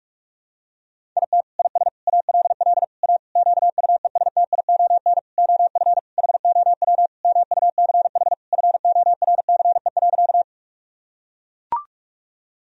Morse code, heard as a keyed tone, about 42 wpm, 700 Hz. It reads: IT IS ALL A QUESTION OF HOW MUCH FORCE.
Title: Sentences from Top 1000 Words - Review Part 3 - 45wpm